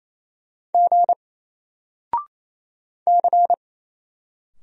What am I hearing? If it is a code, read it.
Morse code, heard as a keyed tone, about 27 wpm, 700 Hz. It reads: G C
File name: Single-Letter-Number-28wpm